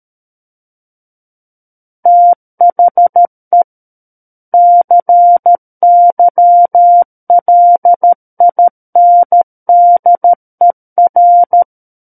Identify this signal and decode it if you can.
Morse code, heard as a keyed tone, about 13 wpm, 700 Hz. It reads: THE CYLINDER
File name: Book One / 04-Chapter 4 - 13 WPM